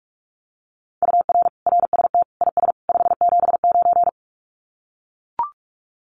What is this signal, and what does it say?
Morse code, heard as a keyed tone, about 44 wpm, 700 Hz. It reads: UR RST IS 579
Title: Common QSO Phrases 45wpm